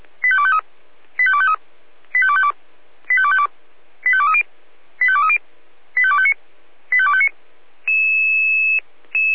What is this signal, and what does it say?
Five-tone selective call (ZVEI), heard as quick runs of five tones, tones of about 70 ms. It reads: 85253 85253 85252 85252 85230 85230 85259 85259